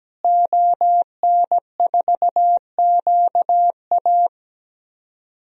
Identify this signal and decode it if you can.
Morse code, heard as a keyed tone, about 17 wpm, 700 Hz. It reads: ON4QA
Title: DXCC-100 Call Signs - 5 Characters - Repeated 2x - 17wpm